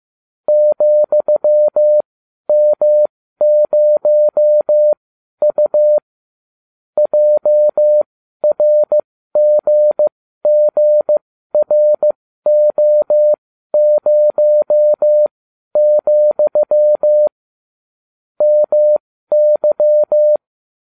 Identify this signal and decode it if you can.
Morse code, heard as a keyed tone, about 15 wpm, spaced as for 12 wpm, 600 Hz. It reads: ,M0U JRGGRO0, MY